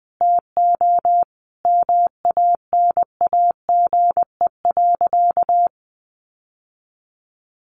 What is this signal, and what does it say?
Morse code, heard as a keyed tone, about 20 wpm, 700 Hz. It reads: TO MANAGE.
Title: Sentences from Top 1400 Words 20wpm